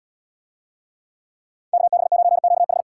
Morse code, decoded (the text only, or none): KB9QH